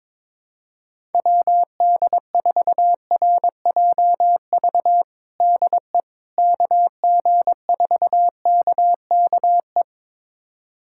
Morse code, heard as a keyed tone, about 22 wpm, 700 Hz. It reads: WD4RJV DE KG4KKE
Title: USA Callsign DE Callsign - 6 Characters - 22wpm